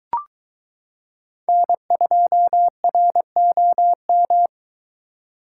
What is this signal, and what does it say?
Morse code, heard as a keyed tone, about 22 wpm, 700 Hz. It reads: N2ROM